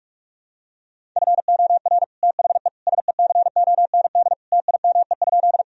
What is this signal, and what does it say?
Morse code, heard as a keyed tone, about 44 wpm, 700 Hz. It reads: FOR THE SECOND TIME?
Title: Sentences from Top 1000 Words - Review Part 1 - 45wpm